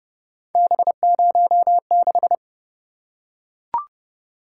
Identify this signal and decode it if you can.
Morse code, heard as a keyed tone, about 27 wpm, 700 Hz. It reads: B06